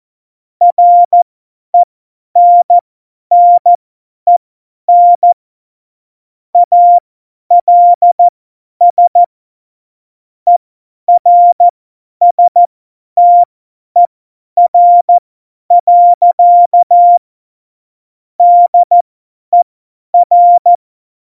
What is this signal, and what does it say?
Morse code, heard as a keyed tone, about 14 wpm, spaced as for 10 wpm, 700 Hz. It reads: RENNEN ALS ERSTER. DER